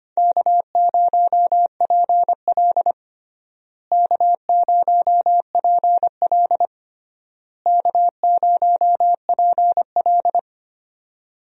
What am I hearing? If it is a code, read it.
Morse code, heard as a keyed tone, about 25 wpm, 700 Hz. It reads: K0PL K0PL K0PL
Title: USA Call Signs - 4 Characters - Repeated 3x - 25wpm